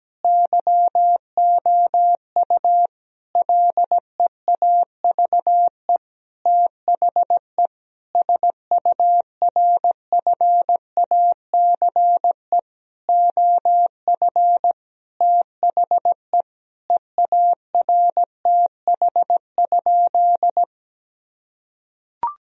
Morse code, heard as a keyed tone, about 17 wpm, 700 Hz. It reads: YOU LEAVE THE SURFACE OF THE EARTH?
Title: Sentences from Top 1000 Words - Review Part 3 - 17wpm